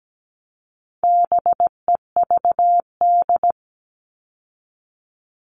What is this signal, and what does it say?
Morse code, heard as a keyed tone, about 17 wpm, 700 Hz. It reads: BEVD
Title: All Cut Numbers - 4-Digit 17wpm